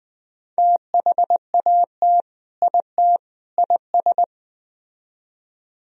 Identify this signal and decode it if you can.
Morse code, heard as a keyed tone, about 20 wpm, 700 Hz. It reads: THAT IT IS